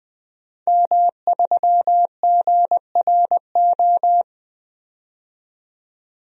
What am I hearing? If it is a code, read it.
Morse code, heard as a keyed tone, about 20 wpm, 700 Hz. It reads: M3GRO
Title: DXCC-100 Call Signs - 5 Characters - 20wpm